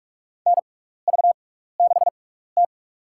Morse code, heard as a keyed tone, about 46 wpm, spaced as for 16 wpm, 700 Hz. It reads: NV6T
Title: USA Call Signs - 4 Characters - 6x Character Spacing - 45wpm